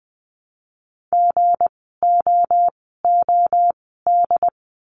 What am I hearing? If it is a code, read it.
Morse code, heard as a keyed tone, about 19 wpm, spaced as for 14 wpm, 700 Hz. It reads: GOOD
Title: 160 Lesson 26f - Letter g - Words 20wpm